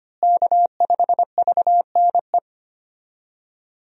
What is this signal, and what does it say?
Morse code, heard as a keyed tone, about 25 wpm, 700 Hz. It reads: K5VNE
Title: USA Call Signs - 5 Characters - Repeated 2x - 25wpm